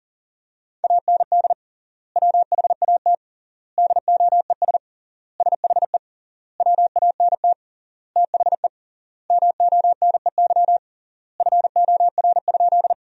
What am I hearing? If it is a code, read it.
Morse code, heard as a keyed tone, about 37 wpm, 700 Hz. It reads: AND WHAT DOES SHE WANT THE MONEY FOR?
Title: 3x Word-Spacing - Sentences from Top 400 Words - 40wpm